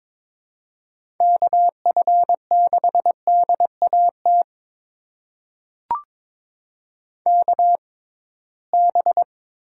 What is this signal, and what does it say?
Morse code, heard as a keyed tone, about 22 wpm, spaced as for 17 wpm, 700 Hz. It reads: KF6DAT KB
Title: USA Call Signs - 6 Characters - 6x Character Spacing - 22wpm